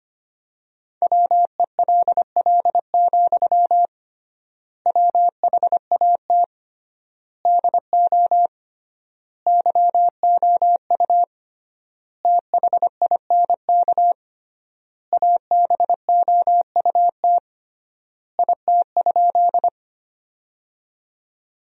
Morse code, heard as a keyed tone, about 25 wpm, 700 Hz. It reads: WELL, WHAT DO YOU THINK ABOUT IT?
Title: 3x Word-Spacing - Sentences from Top 200 Words - 25wpm